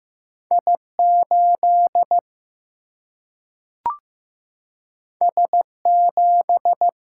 Morse code, heard as a keyed tone, about 15 wpm, 700 Hz. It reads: I8 S7
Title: Single-Letter Single-Number 15wpm